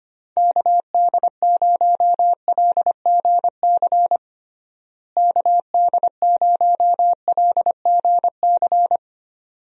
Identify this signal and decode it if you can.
Morse code, heard as a keyed tone, about 25 wpm, 700 Hz. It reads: KD0LGC KD0LGC
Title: USA Call Signs - 6 Characters - Repeated 2x - 25wpm